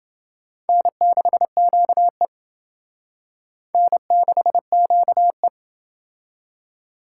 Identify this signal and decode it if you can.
Morse code, heard as a keyed tone, about 28 wpm, 700 Hz. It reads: N6QE N6QE